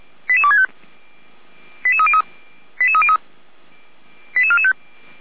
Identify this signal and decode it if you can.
Five-tone selective call (ZVEI), heard as quick runs of five tones, tones of about 70 ms. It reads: 88265 88303 88303 88445
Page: /455/SDR/ZVEI/